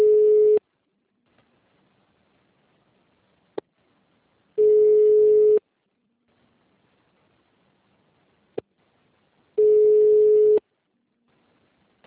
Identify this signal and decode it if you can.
Ringback tone (Europe, 425 Hz).
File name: out- -2002- - - .705